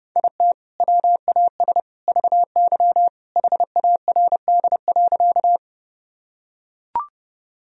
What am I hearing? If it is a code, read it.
Morse code, heard as a keyed tone, about 32 wpm, 700 Hz. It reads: IT WAS VY HARD.
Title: Sentences from Top 300 Words and Abbreviations 30wpm